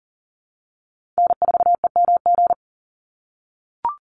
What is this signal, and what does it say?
Morse code, heard as a keyed tone, about 40 wpm, 700 Hz. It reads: N4EMG
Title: NA SOTA Chaser Callsigns 40wpm